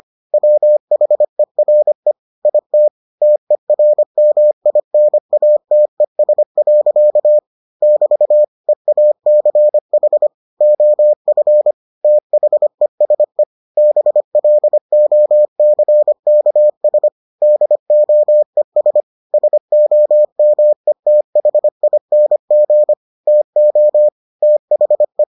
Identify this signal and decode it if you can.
Morse code, heard as a keyed tone, about 25 wpm, 600 Hz. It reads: WHERE IT TERMINATES. <BT> EACH OF THESE BLOCKS DOES SOMETHING TO THE